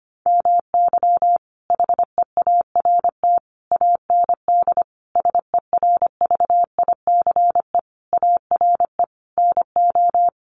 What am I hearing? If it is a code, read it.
Morse code, heard as a keyed tone, about 25 wpm, 700 Hz. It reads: MY HEART AND SERVICE ARE NO
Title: Sentences from Top 1200 Words 25wpm